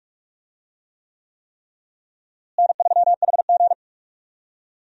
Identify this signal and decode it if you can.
Morse code, heard as a keyed tone, about 46 wpm, 700 Hz. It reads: N3HG
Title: USA Call Signs - 4 Characters - Repeated 3x - 45wpm